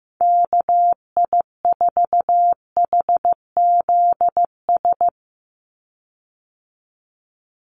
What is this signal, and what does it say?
Morse code, heard as a keyed tone, about 15 wpm, 700 Hz. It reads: KI4HZS
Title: USA Call Signs 15wpm